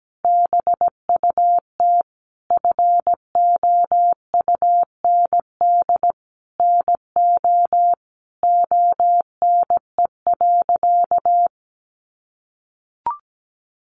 Morse code, heard as a keyed tone, about 17 wpm, 700 Hz. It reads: BUT FOUND NO ONE.
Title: Sentences from Top 1400 Words 17wpm